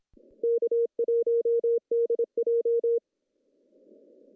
Morse code, heard as a keyed tone, about 27 wpm, 440 Hz. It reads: K1DJ
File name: CW / cqww-cw-k5tr